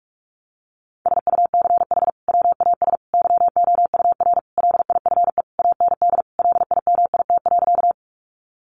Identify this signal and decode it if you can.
Morse code, heard as a keyed tone, about 44 wpm, 700 Hz. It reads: SUCH WAS YOUR LIFE AND LIMIT.
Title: Sentences from Top 1500 Words 45wpm